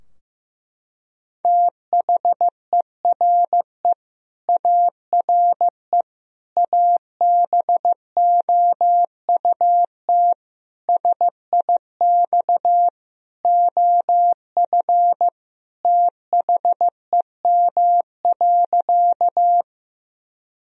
Morse code, heard as a keyed tone, about 15 wpm, 700 Hz. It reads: THERE ARE ABOUT SIX OF THEM.
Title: Sentences from Top 1000 Words - Review Part 2 - 15wpm